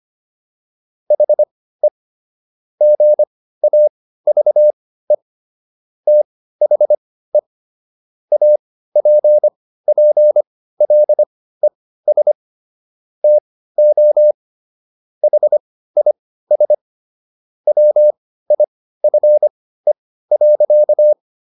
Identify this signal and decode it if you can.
Morse code, heard as a keyed tone, about 25 wpm, spaced as for 15 wpm, 600 Hz. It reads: HE GAVE THE APPLES TO HIS WIFE.